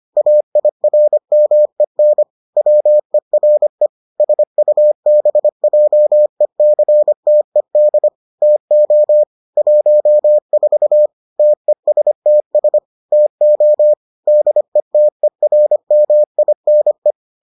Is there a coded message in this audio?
Yes: Morse code from 0.2 to 17.1 s, heard as a keyed tone, about 25 wpm, 600 Hz. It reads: AIRMEN WERE SUBJECTED TO 14 TESTS TO DETERMINE